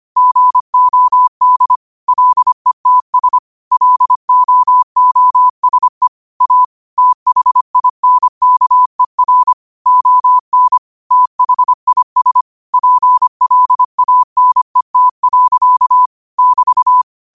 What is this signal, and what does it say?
Morse code, heard as a keyed tone, about 25 wpm, 1000 Hz. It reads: GOD LETS LOOSE A THINKER ON THIS PLANET. <BT>